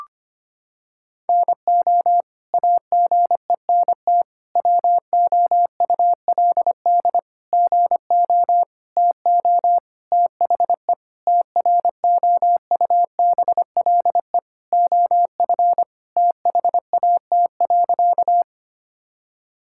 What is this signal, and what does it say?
Morse code, heard as a keyed tone, about 25 wpm, 700 Hz. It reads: NO AGENT WOULD GO TO THE TROUBLE OF THAT.
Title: Sentences from Top 1600 Words 25wpm